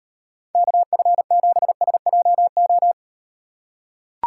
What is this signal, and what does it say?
Morse code, heard as a keyed tone, about 38 wpm, 700 Hz. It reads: KF7SJO